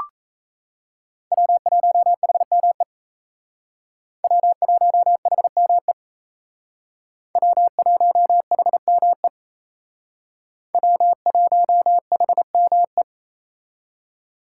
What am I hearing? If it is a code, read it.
Morse code, heard as a keyed tone, about 35 wpm, 700 Hz. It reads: W1HME W1HME W1HME W1HME
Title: Speed Racing - USA 1x3 Call Signs 28wpm